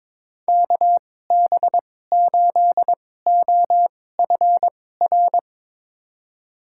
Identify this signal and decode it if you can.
Morse code, heard as a keyed tone, about 22 wpm, spaced as for 16 wpm, 700 Hz. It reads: KB8OFR
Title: USA Call Signs - 6 Characters - 2x Character Spacing - 22wpm